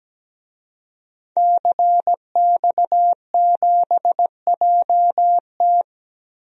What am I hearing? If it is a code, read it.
Morse code, heard as a keyed tone, about 17 wpm, 700 Hz. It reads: CX7JT
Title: DXCC-100 Call Signs - 5 Characters - 17wpm